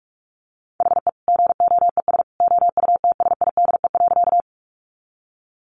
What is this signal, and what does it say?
Morse code, heard as a keyed tone, about 44 wpm, 700 Hz. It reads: HE GOES OUTSIDE.
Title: Additional Easy Phrases 45wpm